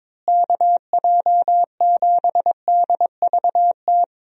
Morse code, heard as a keyed tone, about 22 wpm, 700 Hz. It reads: KJ7DVT
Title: USA Call Signs - 6 Characters - Repeated 3x - 22wpm